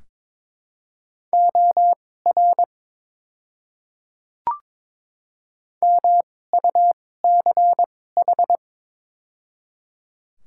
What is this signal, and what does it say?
Morse code, heard as a keyed tone, about 22 wpm, spaced as for 16 wpm, 700 Hz. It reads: OR MUCH
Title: 2x Character Spacing - Top 300 Words - 22wpm